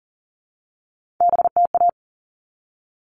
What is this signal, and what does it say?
Morse code, heard as a keyed tone, about 40 wpm, 700 Hz. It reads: BTA
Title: All Cut Numbers - 3-Digit 40wpm